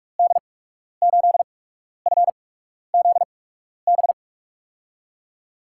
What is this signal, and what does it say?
Morse code, heard as a keyed tone, about 44 wpm, spaced as for 12 wpm, 700 Hz. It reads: D8FZB
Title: USA Call Signs - 6 Characters - 8x Character Spacing - 45wpm